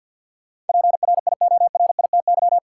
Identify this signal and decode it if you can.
Morse code, heard as a keyed tone, about 52 wpm, 700 Hz. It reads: PRIORITY
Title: Sentences from Top 1600 Words - New Word Drill Alt - 50wpm